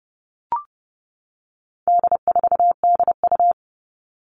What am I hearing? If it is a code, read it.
Morse code, heard as a keyed tone, about 30 wpm, 700 Hz. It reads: D4DU
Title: All Cut Numbers - 4-Digit 30wpm